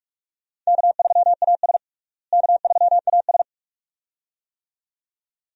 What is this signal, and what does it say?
Morse code, heard as a keyed tone, about 41 wpm, 700 Hz. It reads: K3AS K3AS
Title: USA Call Signs - 4 Characters - Repeated 2x - 45wpm